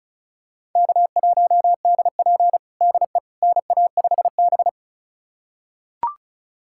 Morse code, heard as a keyed tone, about 36 wpm, 700 Hz. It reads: K1DP DE NA5B
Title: USA Callsign DE Callsign - 4 Characters - 35wpm